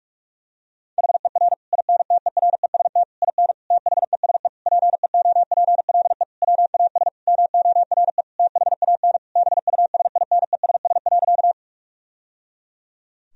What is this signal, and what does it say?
Morse code, heard as a keyed tone, about 46 wpm, 700 Hz. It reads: HER INTEREST IN THESE PEOPLE WAS MORE THAN BUSINESS.
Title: Sentences from Top 1000 Words - Review Part 2 - 45wpm